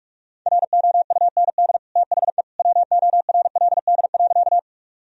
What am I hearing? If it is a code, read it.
Morse code, heard as a keyed tone, about 47 wpm, 700 Hz. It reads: ROUND THE WORLD.
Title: Sentences from Top 1000 Words - Review Part 1 - 45wpm